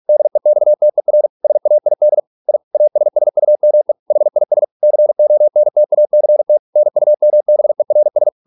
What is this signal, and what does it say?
Morse code, heard as a keyed tone, about 46 wpm, 600 Hz. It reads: DEXTER SAID I ASSUME HIS CONTACT NUMBERS